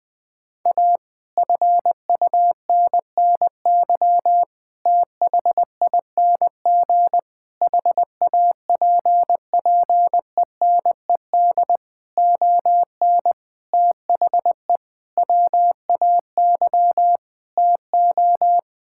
Morse code, heard as a keyed tone, about 20 wpm, 700 Hz. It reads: A FUNNY THING HAPPENED ON THE WAY TO
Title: Sentences with 8-Word Phrases - 20wpm